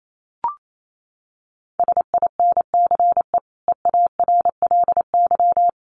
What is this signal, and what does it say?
Morse code, heard as a keyed tone, about 28 wpm, 700 Hz. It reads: SINCE EARLY
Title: Sentences from Top 1400 Words 28wpm